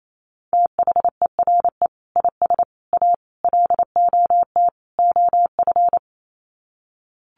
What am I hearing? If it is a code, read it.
Morse code, heard as a keyed tone, about 28 wpm, 700 Hz. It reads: THERE IS A LOT OF
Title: Sets of 5 Words 28wpm